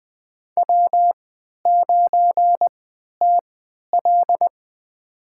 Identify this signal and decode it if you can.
Morse code, heard as a keyed tone, about 21 wpm, spaced as for 11 wpm, 700 Hz. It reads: W9TL
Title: USA Call Signs - 3x Character Spacing - 20wpm